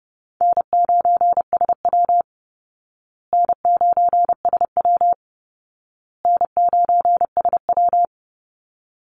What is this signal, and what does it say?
Morse code, heard as a keyed tone, about 30 wpm, 700 Hz. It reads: N9SW N9SW N9SW
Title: Easy Call Signs Repeated 3x - 30wpm